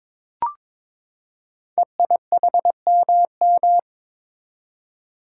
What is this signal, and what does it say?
Morse code, heard as a keyed tone, about 22 wpm, 690 Hz. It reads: EIHMM
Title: Code Groups - 5 Letters - 22wpm